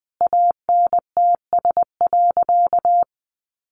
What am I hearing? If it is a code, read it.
Morse code, heard as a keyed tone, about 19 wpm, 700 Hz. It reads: ANTS.
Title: Easy Phrases 20wpm